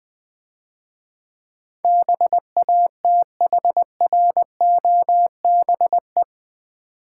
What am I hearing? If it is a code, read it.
Morse code, heard as a keyed tone, about 20 wpm, 700 Hz. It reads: BATHROBE
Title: Difficult Words - Speed-Racing with Hints - 20wpm